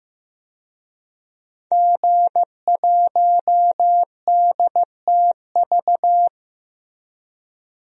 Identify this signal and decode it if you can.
Morse code, heard as a keyed tone, about 15 wpm, 700 Hz. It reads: G1DTV
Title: DXCC-100 Call Signs - 5 Characters - 15wpm